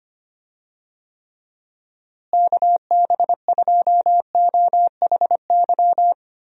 Morse code, heard as a keyed tone, about 25 wpm, 700 Hz. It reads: KB2OHY
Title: USA Call Signs - 6 Characters - Repeated 3x - 25wpm